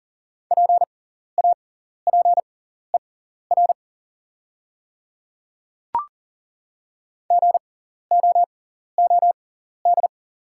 Morse code, heard as a keyed tone, about 40 wpm, spaced as for 14 wpm, 690 Hz. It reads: PAPER GOOD
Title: 6x Character Spacing - Top 300 Words - 40wpm